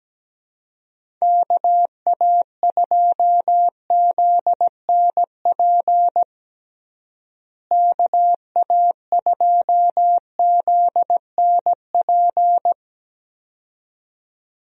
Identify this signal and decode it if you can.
Morse code, heard as a keyed tone, about 17 wpm, 700 Hz. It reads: KA2ZNP KA2ZNP
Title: USA Call Signs - 6 Characters - Repeated 2x - 17wpm